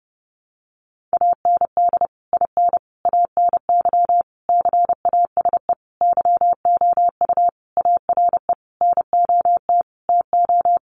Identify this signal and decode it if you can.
Morse code, heard as a keyed tone, about 30 wpm, 700 Hz. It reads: AND IN ANY CASE YOU ARE NOT TO
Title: Sentences from Top 1200 Words 30wpm